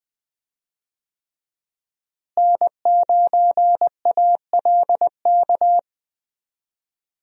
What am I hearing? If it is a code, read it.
Morse code, heard as a keyed tone, about 20 wpm, 700 Hz. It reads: N9ALK
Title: USA Call Signs - 5 Characters - Repeated 3x - 20wpm